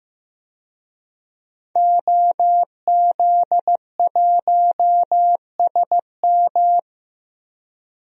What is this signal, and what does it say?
Morse code, heard as a keyed tone, about 15 wpm, 700 Hz. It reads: OZ1SM
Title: DXCC-100 Call Signs - 5 Characters - 15wpm